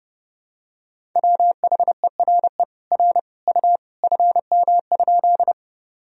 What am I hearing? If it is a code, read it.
Morse code, heard as a keyed tone, about 30 wpm, 700 Hz. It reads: WHERE R U FM?
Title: Sentences from Top 200 Words and Abbreviations 30wpm